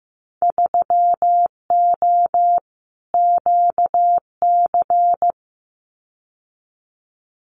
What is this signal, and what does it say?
Morse code, heard as a keyed tone, about 15 wpm, 700 Hz. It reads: 3O QC
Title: Winter Field Day Exchanges Encore 15wpm